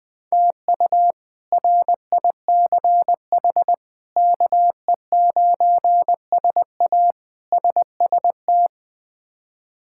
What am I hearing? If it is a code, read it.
Morse code, heard as a keyed tone, about 20 wpm, 700 Hz. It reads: TU RICH KE9SA SST
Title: K1USN SST QSO Practice - 20wpm